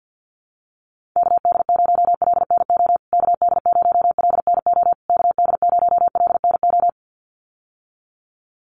Morse code, heard as a keyed tone, about 50 wpm, 700 Hz. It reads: KD0LNO KD0LNO KD0LNO
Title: USA Call Signs Repeated 3x 50wpm